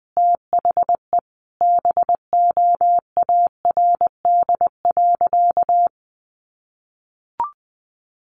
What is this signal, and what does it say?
Morse code, heard as a keyed tone, about 20 wpm, 700 Hz. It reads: THE BOARD.
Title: Sentences from Top 1100 Words 20wpm